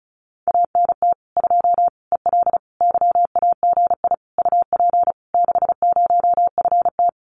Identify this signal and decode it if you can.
Morse code, heard as a keyed tone, about 34 wpm, 700 Hz. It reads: ANT 2 EL YAGI UP 60FT
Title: Common QSO Phrases Encore 35wpm